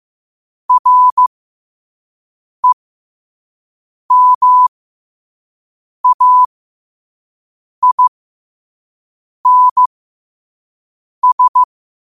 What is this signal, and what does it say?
Morse code, heard as a keyed tone, about 15 wpm, spaced as for 5 wpm, 1000 Hz. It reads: REMAINS
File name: qotdfiles / 5wpm